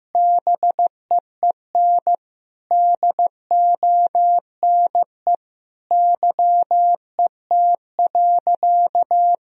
Morse code, heard as a keyed tone, about 15 wpm, 700 Hz. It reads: BEEN DONE YET.
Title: Sentences from Top 1000 Words - Review Part 2 - 15wpm